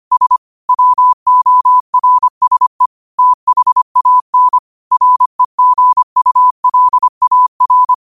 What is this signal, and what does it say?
Morse code, heard as a keyed tone, about 25 wpm, 1000 Hz. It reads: S WORSE THAN REGULAR
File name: 25wpm